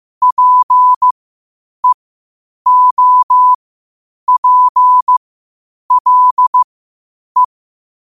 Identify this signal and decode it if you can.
Morse code, heard as a keyed tone, about 15 wpm, spaced as for 8 wpm, 1000 Hz. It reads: PEOPLE
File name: qotdfiles / 8wpm